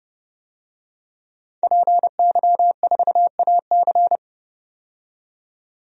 Morse code, heard as a keyed tone, about 30 wpm, 700 Hz. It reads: PY4AC
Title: DXCC-100 Call Signs - 5 Characters - 30wpm